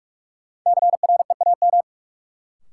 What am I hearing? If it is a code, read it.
Morse code, heard as a keyed tone, about 46 wpm, 700 Hz. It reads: CREAM
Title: Sentences from Top 1500 Words - New Word Drill - 45wpm